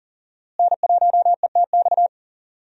Morse code, heard as a keyed tone, about 40 wpm, 700 Hz. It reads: N1ETX
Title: USA Call Signs - 5 Characters - Repeated 3x - 40wpm